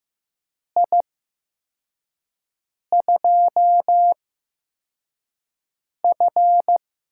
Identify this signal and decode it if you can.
Morse code, heard as a keyed tone, about 15 wpm, spaced as for 4 wpm, 700 Hz. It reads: I2F
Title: USA Call Signs - 8x Character Spacing - 15wpm